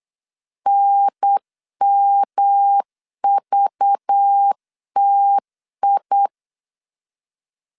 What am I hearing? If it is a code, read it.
Morse code, heard as a keyed tone, about 8 wpm, 800 Hz. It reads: NMVTI
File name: LEZIONI / 7 GF9Z / 7 GF9 8wpm 3545